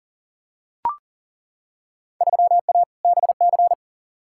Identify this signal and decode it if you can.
Morse code, heard as a keyed tone, about 40 wpm, 700 Hz. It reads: 3A BC
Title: Field Day Exchanges - 40wpm